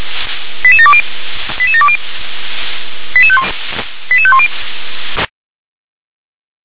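Five-tone selective call (ZVEI), heard as quick runs of five tones, tones of about 70 ms. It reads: 88720 88720 88510
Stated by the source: ZVEI five-tone call 88730 received on an SDR.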